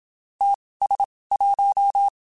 Morse code, heard as a keyed tone, about 27 wpm, spaced as for 19 wpm, 800 Hz. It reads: TS1
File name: QSOs / QSOs-mp3s / WD0FBT 1@25